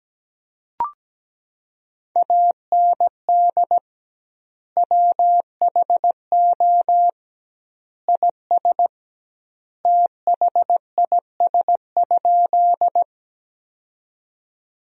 Morse code, heard as a keyed tone, about 17 wpm, 700 Hz. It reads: AND WHO IS THIS?
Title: 2x Word-Spacing - Sentences from Top 100 Words - 17wpm